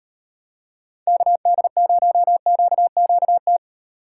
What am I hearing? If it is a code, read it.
Morse code, heard as a keyed tone, about 32 wpm, 700 Hz. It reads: KD0QQT